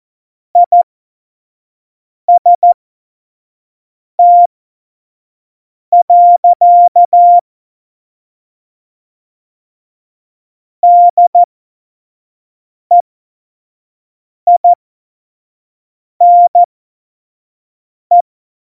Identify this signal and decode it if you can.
Morse code, heard as a keyed tone, about 14 wpm, spaced as for 5 wpm, 700 Hz. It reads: IST. DEINE
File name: Morsekurs / Witze / Witz 26. Februar 2019, 25 F70